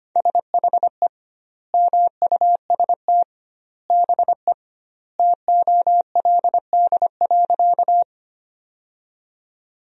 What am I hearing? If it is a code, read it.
Morse code, heard as a keyed tone, about 25 wpm, 700 Hz. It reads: SHE MUST BE TOLD.
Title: 2x Word-Spacing - Sentences from Top 400 Words - 25wpm